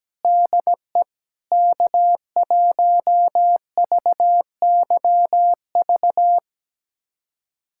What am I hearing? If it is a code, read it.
Morse code, heard as a keyed tone, about 17 wpm, 700 Hz. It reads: DE K1VYV
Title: USA Callsign DE Callsign 17wpm